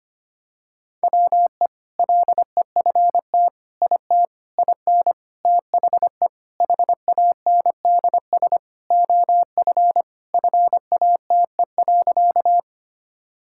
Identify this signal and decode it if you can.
Morse code, heard as a keyed tone, about 25 wpm, 700 Hz. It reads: WE LEFT IT IN THE HANDS OF FATE.
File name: practice / 25wpm / Sentences with 4-Word Phrases - 25wpm